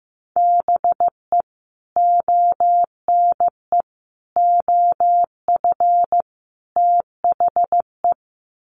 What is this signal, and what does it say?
Morse code, heard as a keyed tone, about 15 wpm, 700 Hz. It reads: BE ONE OF THE
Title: Sets of 4 Words 15wpm